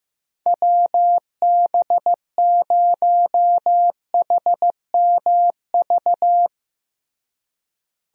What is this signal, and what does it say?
Morse code, heard as a keyed tone, about 15 wpm, 700 Hz. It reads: WB0HMV
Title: USA Call Signs - 6 Characters - Repeated 3x - 15wpm